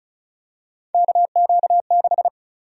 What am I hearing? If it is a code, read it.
Morse code, heard as a keyed tone, about 36 wpm, 700 Hz. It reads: KQ6
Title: USA Call Sign Prefixes - 35wpm